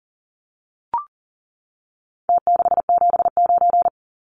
Morse code, heard as a keyed tone, about 40 wpm, 700 Hz. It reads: T679
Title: Numbers - Contest Style - 4-Digit 40wpm